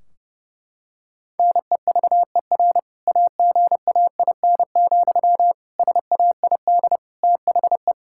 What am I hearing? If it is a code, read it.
Morse code, heard as a keyed tone, about 32 wpm, 700 Hz. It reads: NEVER AGAIN, SAID THE
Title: Sentences from Top 1600 Words 30wpm